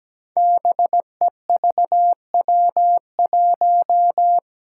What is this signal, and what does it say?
Morse code, heard as a keyed tone, about 17 wpm, 700 Hz. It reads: BEVW1
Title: Five Challenging Letter-Number Combinations 17wpm